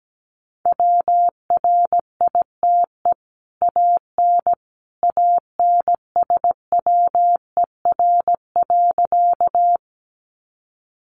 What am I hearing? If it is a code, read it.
Morse code, heard as a keyed tone, about 17 wpm, 700 Hz. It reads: WRITE AN ANSWER.
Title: Sentences from Top 300 Words 17wpm